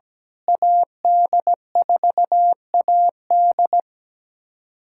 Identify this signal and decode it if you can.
Morse code, heard as a keyed tone, about 17 wpm, 700 Hz. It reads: AD4AD
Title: USA Callsign DE Callsign - 5 Characters - 17wpm